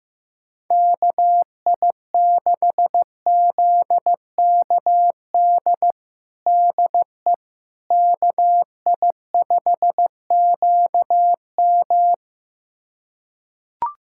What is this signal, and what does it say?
Morse code, heard as a keyed tone, about 15 wpm, 700 Hz. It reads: KI6ZKD DE KI5QM
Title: USA Callsign DE Callsign 15wpm